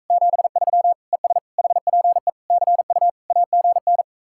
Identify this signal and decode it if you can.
Morse code, heard as a keyed tone, about 45 wpm, 700 Hz. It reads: 73 ES HPE CU AGN